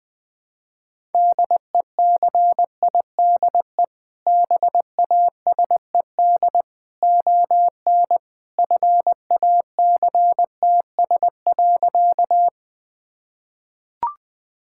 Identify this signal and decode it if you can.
Morse code, heard as a keyed tone, about 20 wpm, 700 Hz. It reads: DECIDE BASED ON FACTS.
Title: Sentences with 2-Word Phrases - Spoken Beginning - 20wpm